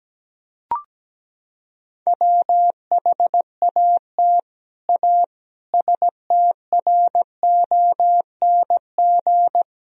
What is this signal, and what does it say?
Morse code, heard as a keyed tone, about 17 wpm, 700 Hz. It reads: WHAT A STRONG
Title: Sentences from Top 1000 Words - Review Part 3 - 17wpm